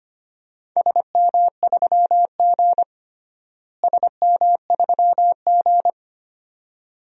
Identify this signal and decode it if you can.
Morse code, heard as a keyed tone, about 25 wpm, 700 Hz. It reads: SM3G SM3G
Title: DXCC-100 Call Signs - 4 Characters - Repeated 2x - 25wpm